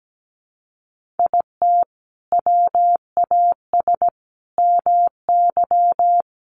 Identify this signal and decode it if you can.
Morse code, heard as a keyed tone, about 17 wpm, 700 Hz. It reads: IT WAS MY
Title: Sentences from Top 1500 Words 17wpm